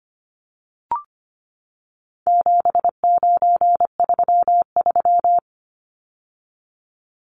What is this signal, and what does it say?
Morse code, heard as a keyed tone, about 25 wpm, 700 Hz. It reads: 7933
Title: Numbers - 4-Digit 25wpm